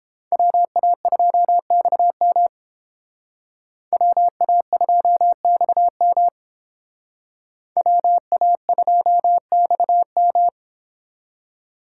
Morse code, heard as a keyed tone, about 30 wpm, 700 Hz. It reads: WA2XM WA2XM WA2XM